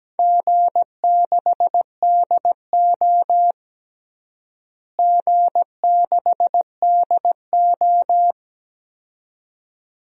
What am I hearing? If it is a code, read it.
Morse code, heard as a keyed tone, about 17 wpm, 700 Hz. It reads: G6DO G6DO
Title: DXCC-100 Call Signs - 4 Characters - Repeated 2x - 17wpm